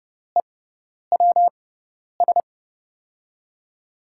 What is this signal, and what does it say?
Morse code, heard as a keyed tone, about 30 wpm, spaced as for 10 wpm, 700 Hz. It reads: EWS
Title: USA Call Signs - 5 Characters - 6x Character Spacing - 30wpm